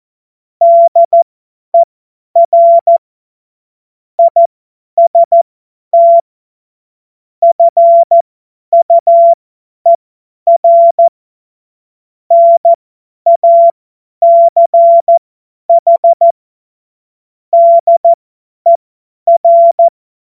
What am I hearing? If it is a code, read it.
Morse code, heard as a keyed tone, about 14 wpm, spaced as for 10 wpm, 680 Hz. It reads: DER IST FUER NACH DER
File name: Morsekurs / Witze / Witz 16. Februar 2021, 50 F70